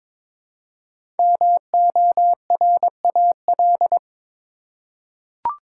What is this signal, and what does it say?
Morse code, heard as a keyed tone, about 22 wpm, 700 Hz. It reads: MORAL
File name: 5 Letter Words - First 3 Letters Given - 22wpm